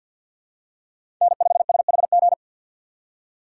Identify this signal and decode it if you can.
Morse code, heard as a keyed tone, about 47 wpm, 700 Hz. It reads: N5SHG
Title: USA Call Signs - 6x Character Spacing - 50wpm